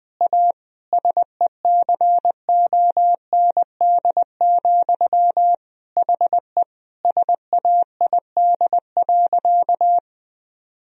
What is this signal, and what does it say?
Morse code, heard as a keyed tone, about 20 wpm, 700 Hz. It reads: A SECOND, HE SAID.
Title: Sentences from Top 1000 Words - Review Part 1 - 20wpm